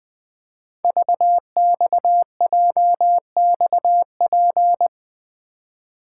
Morse code, heard as a keyed tone, about 20 wpm, 700 Hz. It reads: VXJXP
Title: Code Groups - 5 Letter:Numbers - 20wpm